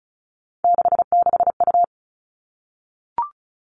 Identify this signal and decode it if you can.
Morse code, heard as a keyed tone, about 34 wpm, 700 Hz. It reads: 66U
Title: All Cut Numbers - 3-Digit 35wpm